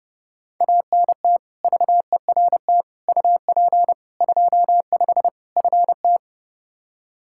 Morse code, heard as a keyed tone, about 30 wpm, 700 Hz. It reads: ANT VERT UP 25 FT
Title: Practice QSO - Exchange 2 - Info - 30wpm